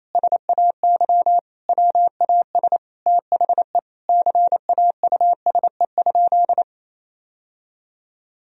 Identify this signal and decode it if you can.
Morse code, heard as a keyed tone, about 28 wpm, 700 Hz. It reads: SAY WAS THE CAUSE?
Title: Sentences from Top 1000 Words - Review Part 1 - 28wpm